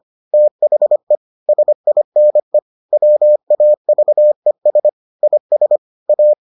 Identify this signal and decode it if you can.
Morse code, heard as a keyed tone, about 25 wpm, 600 Hz. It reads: THE SINE WAVES IS A